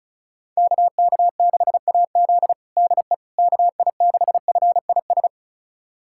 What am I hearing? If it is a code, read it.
Morse code, heard as a keyed tone, about 34 wpm, 700 Hz. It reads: KK6AZ DE KI6FIS
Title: USA Callsign DE Callsign 35wpm